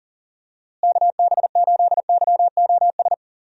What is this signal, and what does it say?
Morse code, heard as a keyed tone, about 39 wpm, 700 Hz. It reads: KB8YOS
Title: USA Call Signs - 6 Characters - Repeated 2x - 40wpm